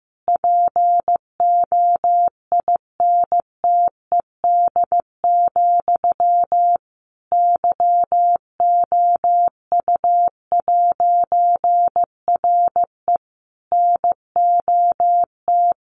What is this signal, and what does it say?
Morse code, heard as a keyed tone, about 15 wpm, 700 Hz. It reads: POINTED, YOU'RE NOT
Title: A Sound Of Thunder 15wpm